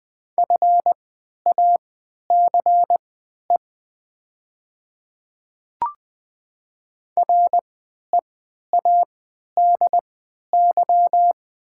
Morse code, heard as a keyed tone, about 20 wpm, spaced as for 11 wpm, 700 Hz. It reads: FACE READY
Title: 3x Character Spacing - Top 500 Words - 20wpm